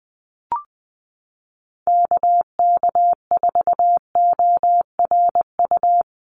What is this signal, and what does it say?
Morse code, heard as a keyed tone, about 20 wpm, 700 Hz. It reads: KK4ORU
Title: USA Call Signs 20wpm